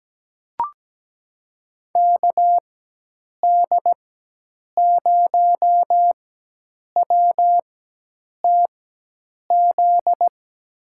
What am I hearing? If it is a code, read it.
Morse code, heard as a keyed tone, about 17 wpm, spaced as for 8 wpm, 700 Hz. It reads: KD0WTZ
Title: USA Call Signs - 6 Characters - 4x Character Spacing - 17wpm